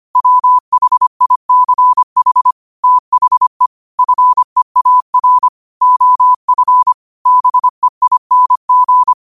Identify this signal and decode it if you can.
Morse code, heard as a keyed tone, about 25 wpm, 1000 Hz. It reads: WHICH THE FEAR OF BEING
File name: qotdfiles / 25wpm